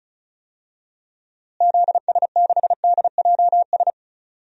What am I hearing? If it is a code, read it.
Morse code, heard as a keyed tone, about 34 wpm, 700 Hz. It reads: ZS6DJS
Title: DXCC-100 Call Signs - 6 Characters - Repeated 2x - 35wpm